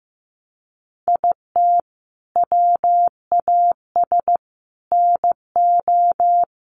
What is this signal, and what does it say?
Morse code, heard as a keyed tone, about 15 wpm, 700 Hz. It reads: IT WAS NO
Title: Sentences from Top 1500 Words 15wpm